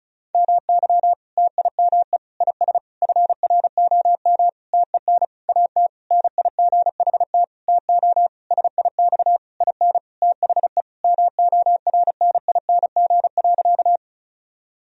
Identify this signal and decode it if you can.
Morse code, heard as a keyed tone, about 34 wpm, 700 Hz. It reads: MY TIME IS FROM TEN AT NIGHT TO SIX IN THE MORNING.
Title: Sentences from Top 1000 Words - Review Part 2 - 35wpm